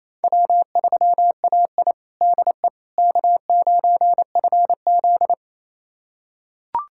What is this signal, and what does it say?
Morse code, heard as a keyed tone, about 28 wpm, 690 Hz. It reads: W3AI DE K9FZ
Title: USA Callsign DE Callsign - 4 Characters - 28wpm